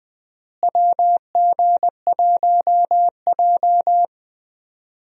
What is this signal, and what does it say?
Morse code, heard as a keyed tone, about 20 wpm, 700 Hz. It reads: WG1J
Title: USA Call Signs - 4 Characters - Repeated 3x - 20wpm